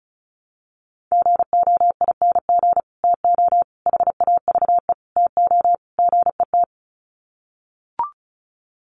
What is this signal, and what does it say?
Morse code, heard as a keyed tone, about 34 wpm, 700 Hz. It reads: GOING TO HAVE TO GET
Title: Sets of 5 Words Encore2 35wpm